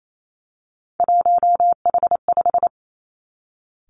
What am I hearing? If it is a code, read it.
Morse code, heard as a keyed tone, about 28 wpm, 700 Hz. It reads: 1H5
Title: Three Challenging Letter-Number Combinations 28wpm